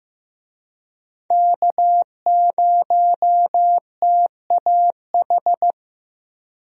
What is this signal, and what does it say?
Morse code, heard as a keyed tone, about 15 wpm, 700 Hz. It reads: K0TAH
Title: USA Call Signs - 5 Characters - Repeated 3x - 15wpm